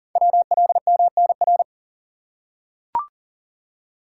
Morse code, heard as a keyed tone, about 42 wpm, 690 Hz. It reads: WLMNR
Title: Code Groups - 5 Letters - 40wpm